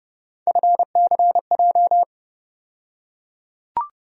Morse code, heard as a keyed tone, about 30 wpm, 700 Hz. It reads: FCJ
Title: USA Call Sign Postfixes - 3 Letters 30wpm